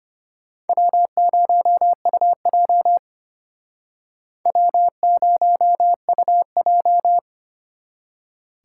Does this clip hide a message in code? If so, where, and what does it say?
Yes: Morse code from 0.7 to 7.2 s, heard as a keyed tone, about 33 wpm, 700 Hz. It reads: W0UJ W0UJ